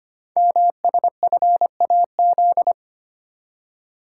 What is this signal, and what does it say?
Morse code, heard as a keyed tone, about 25 wpm, 700 Hz. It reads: MSFAZ
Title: Code Groups - 5 Letters - 25wpm